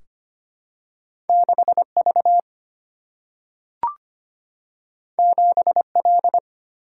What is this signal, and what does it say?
Morse code, heard as a keyed tone, about 25 wpm, 700 Hz. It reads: 6 V 7 L
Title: Single-Letter Single-Number 25wpm